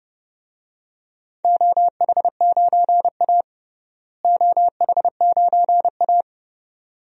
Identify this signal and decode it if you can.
Morse code, heard as a keyed tone, about 30 wpm, 700 Hz. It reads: OH9A OH9A
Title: DXCC-100 Call Signs - 4 Characters - Repeated 2x - 30wpm